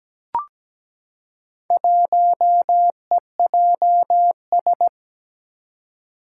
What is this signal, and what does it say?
Morse code, heard as a keyed tone, about 17 wpm, 700 Hz. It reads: 1EJS
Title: Four Challenging Letter-Number Combinations 17wpm